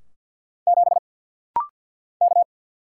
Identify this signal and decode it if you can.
Morse code, heard as a keyed tone, about 50 wpm, 700 Hz. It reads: 7 K
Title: Single Letter-Number - Rapid-Fire 50wpm